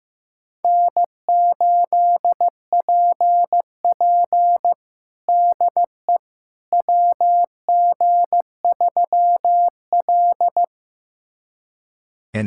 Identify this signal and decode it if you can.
Morse code, heard as a keyed tone, about 15 wpm, 700 Hz. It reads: N8PP DE WG3L
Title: USA Callsign DE Callsign - 4 Characters - 15wpm